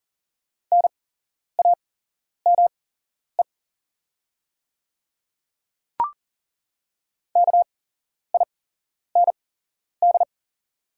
Morse code, heard as a keyed tone, about 40 wpm, spaced as for 11 wpm, 690 Hz. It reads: NAME KIND
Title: 8x Character Spacing - Top 200 Words - 40wpm